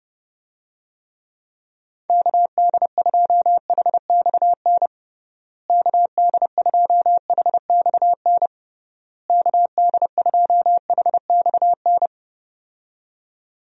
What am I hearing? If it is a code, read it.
Morse code, heard as a keyed tone, about 30 wpm, 700 Hz. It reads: KD2HXN KD2HXN KD2HXN
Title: USA Call Signs - 6 Characters - Repeated 3x - 30wpm